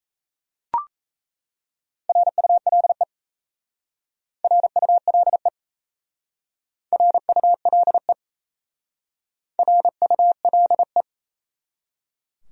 Morse code, heard as a keyed tone, about 34 wpm, 700 Hz. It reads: RULE RULE RULE RULE
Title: Speed-Racing Top 500 Words 28wpm